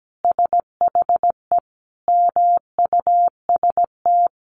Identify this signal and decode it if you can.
Morse code, heard as a keyed tone, about 17 wpm, 700 Hz. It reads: SHE MUST
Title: Sentences from Top 1200 Words 17wpm